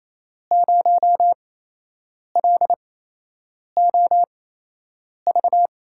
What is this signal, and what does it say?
Morse code, heard as a keyed tone, about 27 wpm, spaced as for 8 wpm, 700 Hz. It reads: 0LOV
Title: USA Call Signs - 6 Characters - 8x Character Spacing - 28wpm